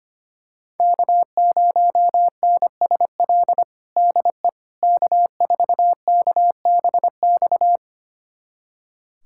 Morse code, heard as a keyed tone, about 25 wpm, 700 Hz. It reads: K0NSL DE K4KBX
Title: USA Callsign DE Callsign - 5 Characters - 25wpm